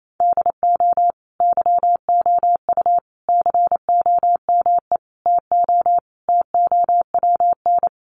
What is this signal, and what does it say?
Morse code, heard as a keyed tone, about 28 wpm, 700 Hz. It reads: DO YOU COME TO TOWN
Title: Sentences from Top 1400 Words 28wpm